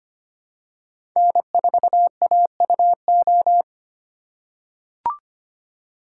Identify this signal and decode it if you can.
Morse code, heard as a keyed tone, about 25 wpm, 700 Hz. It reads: N4AUO
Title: USA Call Signs - 5 Characters - 4x Character Spacing - 25wpm